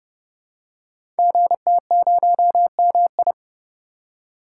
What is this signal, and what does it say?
Morse code, heard as a keyed tone, about 29 wpm, 700 Hz. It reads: GT0MI
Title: Code Groups - 5 Letter:Numbers - 30wpm